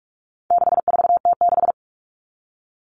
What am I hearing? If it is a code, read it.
Morse code, heard as a keyed tone, about 44 wpm, 700 Hz. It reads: 64T6
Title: Common Cut Numbers - 4-Digit 45wpm